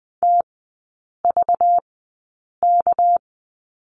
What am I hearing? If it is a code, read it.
Morse code, heard as a keyed tone, about 20 wpm, spaced as for 8 wpm, 690 Hz. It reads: TVK
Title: 198 Lesson 32e - Letter k - Three Characters 20wpm